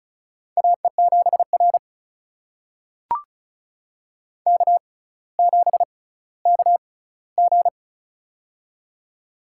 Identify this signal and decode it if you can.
Morse code, heard as a keyed tone, about 36 wpm, spaced as for 18 wpm, 700 Hz. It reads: AE7R K7KG
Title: USA Call Signs - 4 Characters - 6x Character Spacing - 35wpm